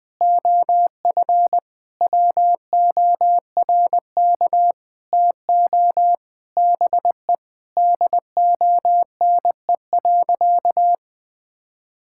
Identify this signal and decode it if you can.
Morse code, heard as a keyed tone, about 21 wpm, 700 Hz. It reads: OF WORK TO BE DONE.
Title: Sentences with 9-Word Phrases - 20wpm